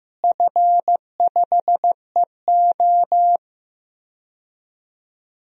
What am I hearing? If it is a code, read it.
Morse code, heard as a keyed tone, about 15 wpm, 700 Hz. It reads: F5EO
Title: DXCC-100 Call Signs - 4 Characters - Repeated 3x - 15wpm